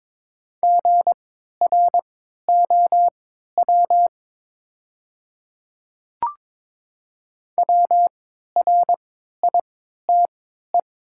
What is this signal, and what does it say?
Morse code, heard as a keyed tone, about 22 wpm, spaced as for 12 wpm, 700 Hz. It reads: GROW WRITE
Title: 3x Character Spacing - Top 500 Words - 22wpm